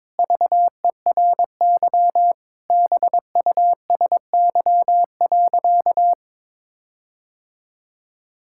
Morse code, heard as a keyed tone, about 22 wpm, 700 Hz. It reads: VERY BUSY.
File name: Sentences with 7-Word Phrases - 22wpm